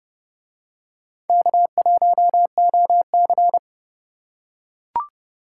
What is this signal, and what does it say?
Morse code, heard as a keyed tone, about 29 wpm, 700 Hz. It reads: K1OC